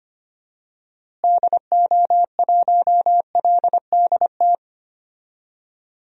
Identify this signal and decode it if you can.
Morse code, heard as a keyed tone, about 25 wpm, 700 Hz. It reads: DO1LDT
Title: DXCC-100 Call Signs - 6 Characters - Repeated 2x - 25wpm